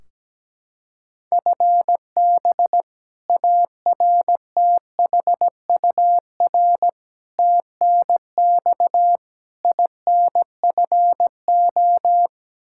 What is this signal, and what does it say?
Morse code, heard as a keyed tone, about 17 wpm, 700 Hz. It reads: FB ARTHUR TNX INFO
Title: Practice QSO - Exchange 3 - More Info - 17wpm